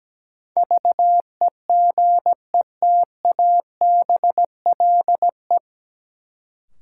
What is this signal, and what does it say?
Morse code, heard as a keyed tone, about 17 wpm, 700 Hz. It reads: VEGETABLE
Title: Sentences from Top 1300 Words - New Word Drill - 17wpm